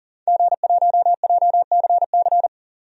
Morse code, heard as a keyed tone, about 40 wpm, 700 Hz. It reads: G1JCC
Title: DXCC-100 Call Signs - 5 Characters - 40wpm